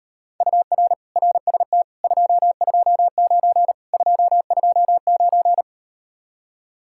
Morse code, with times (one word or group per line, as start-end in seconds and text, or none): 0.4-0.9 UR
1.2-1.8 RST
2.0-3.7 229
3.9-5.6 229